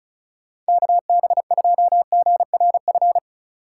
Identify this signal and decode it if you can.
Morse code, heard as a keyed tone, about 36 wpm, 700 Hz. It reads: KB2GRF
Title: USA Call Signs - 6 Characters - 4x Character Spacing - 35wpm